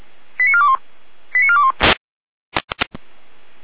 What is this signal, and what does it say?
Five-tone selective call (ZVEI), heard as quick runs of five tones, tones of about 70 ms. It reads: 89421 89421